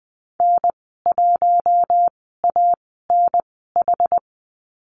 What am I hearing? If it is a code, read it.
Morse code, heard as a keyed tone, about 20 wpm, spaced as for 14 wpm, 700 Hz. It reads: N1ANH
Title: 062 Lesson 11g - Letter h - Call Signs 20wpm